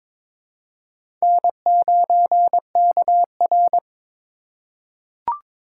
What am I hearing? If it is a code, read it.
Morse code, heard as a keyed tone, about 22 wpm, 700 Hz. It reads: N9KR
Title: K1USN SST QSO Practice - 22wpm